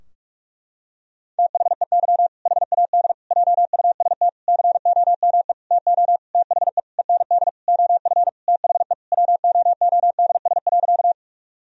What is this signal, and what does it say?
Morse code, heard as a keyed tone, about 46 wpm, 700 Hz. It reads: THEY HAD JUST COME TO THE END OF THE WOODS.
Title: Sentences from Top 1000 Words - Review Part 2 - 45wpm